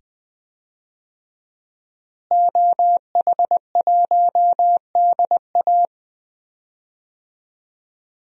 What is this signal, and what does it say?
Morse code, heard as a keyed tone, about 21 wpm, 700 Hz. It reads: OH1DA
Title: DXCC-100 Call Signs - 5 Characters - Repeated 2x - 20wpm